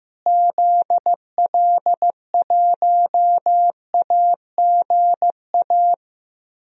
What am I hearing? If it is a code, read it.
Morse code, heard as a keyed tone, about 15 wpm, 700 Hz. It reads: ZL1AGA
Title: DXCC-100 Call Signs - 6 Characters - Repeated 3x - 15wpm